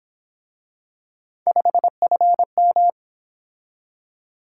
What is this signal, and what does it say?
Morse code, heard as a keyed tone, about 26 wpm, 700 Hz. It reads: 5FM
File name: Speed-Racing - Three Letter-Number Combinations - 22wpm